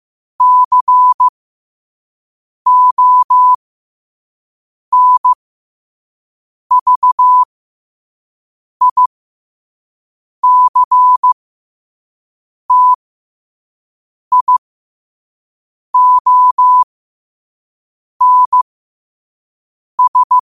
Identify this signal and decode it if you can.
Morse code, heard as a keyed tone, about 15 wpm, spaced as for 5 wpm, 1000 Hz. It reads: CONVICTIONS
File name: qotdfiles / 5wpm